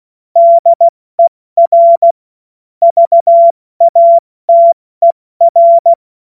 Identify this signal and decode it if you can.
Morse code, heard as a keyed tone, about 16 wpm, 680 Hz. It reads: DER VATER
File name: Witz-08-Juli-2025-70-F80